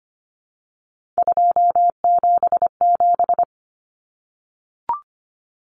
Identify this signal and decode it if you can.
Morse code, heard as a keyed tone, about 25 wpm, 700 Hz. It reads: 277
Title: Common Cut Numbers - 3-Digit 25wpm